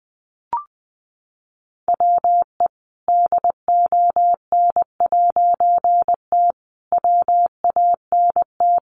Morse code, heard as a keyed tone, about 20 wpm, 700 Hz. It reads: WE DON'T WANT
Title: A Sound Of Thunder 20wpm